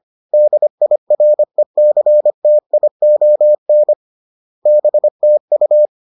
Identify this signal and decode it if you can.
Morse code, heard as a keyed tone, about 25 wpm, 600 Hz. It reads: DIRECTION BTU
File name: 25wpm-6